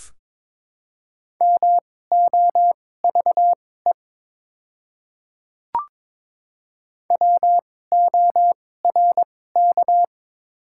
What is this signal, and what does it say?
Morse code, heard as a keyed tone, about 21 wpm, spaced as for 16 wpm, 690 Hz. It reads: MOVE WORK
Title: 2x Character Spacing - Top 200 Words - 22wpm